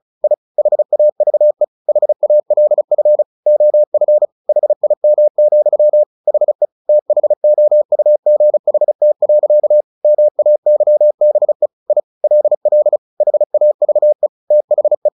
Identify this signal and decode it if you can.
Morse code, heard as a keyed tone, about 34 wpm, 600 Hz. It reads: I HAVE HALF OF HIM, HE THOUGHT. MAYBE I LL HAVE THE
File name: the old man and the sea 2 35wpm